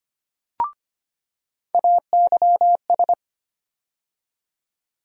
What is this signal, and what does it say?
Morse code, heard as a keyed tone, about 25 wpm, 700 Hz. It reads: AYS
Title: USA Call Sign Postfixes - 3 Letters 25wpm